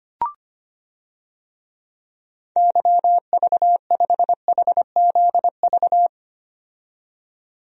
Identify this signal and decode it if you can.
Morse code, heard as a keyed tone, about 25 wpm, 700 Hz. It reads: YV5HZV
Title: DXCC-100 Call Signs - 6 Characters - 25wpm